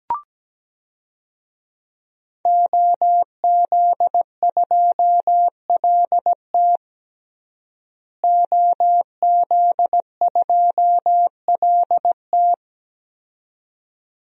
Morse code, heard as a keyed tone, about 17 wpm, 700 Hz. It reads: OZ2LT OZ2LT
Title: DXCC-100 Call Signs - 5 Characters - Repeated 2x - 17wpm